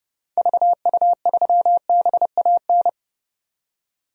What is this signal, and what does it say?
Morse code, heard as a keyed tone, about 32 wpm, 700 Hz. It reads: VU3BAN
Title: DXCC-100 Call Signs - 6 Characters - 30wpm